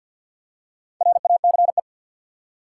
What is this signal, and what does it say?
Morse code, heard as a keyed tone, about 52 wpm, 700 Hz. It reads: RACE
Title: Four Letter Words - 50wpm